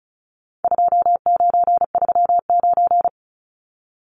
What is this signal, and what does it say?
Morse code, heard as a keyed tone, about 34 wpm, 700 Hz. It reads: 2939
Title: Numbers - 4-Digit 35wpm